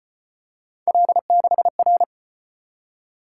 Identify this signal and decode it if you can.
Morse code, heard as a keyed tone, about 33 wpm, 700 Hz. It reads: L6R